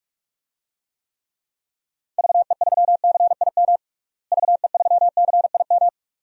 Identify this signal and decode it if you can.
Morse code, heard as a keyed tone, about 45 wpm, 700 Hz. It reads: VE3CIM VE3CIM
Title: DXCC-100 Call Signs - 6 Characters - Repeated 2x - 45wpm